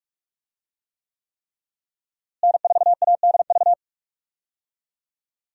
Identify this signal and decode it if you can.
Morse code, heard as a keyed tone, about 46 wpm, 700 Hz. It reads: N4ADV
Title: USA Call Signs - 5 Characters - Repeated 3x - 45wpm